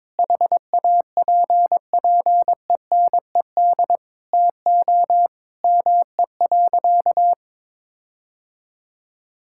Morse code, heard as a keyed tone, about 22 wpm, 700 Hz. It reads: HAPPENED TO ME.
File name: practice / 22wpm / Sentences with 8-Word Phrases - 22wpm